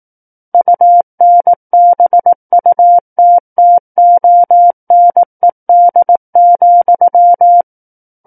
Morse code, heard as a keyed tone, about 18 wpm, 700 Hz. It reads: UNBUTTONED,
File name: 18 WPM